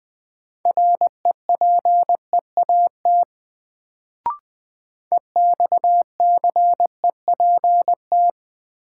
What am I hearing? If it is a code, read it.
Morse code, heard as a keyed tone, about 20 wpm, 700 Hz. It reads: REPEAT EXCEPT
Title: Sentences from Top 1600 Words - New Word Drill - 20wpm